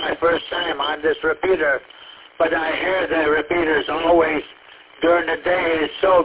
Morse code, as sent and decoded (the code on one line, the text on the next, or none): none